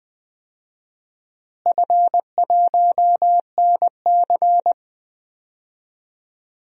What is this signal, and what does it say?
Morse code, heard as a keyed tone, about 20 wpm, 700 Hz. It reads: F1NC
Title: DXCC-100 Call Signs - 4 Characters - Repeated 2x - 20wpm